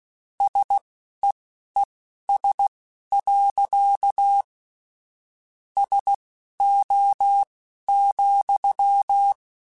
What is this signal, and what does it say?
Morse code, heard as a keyed tone, about 16 wpm, spaced as for 11 wpm, 800 Hz. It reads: SEES. SO,
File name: QSOs / QSOs-mp3s / WD0FBT 1@10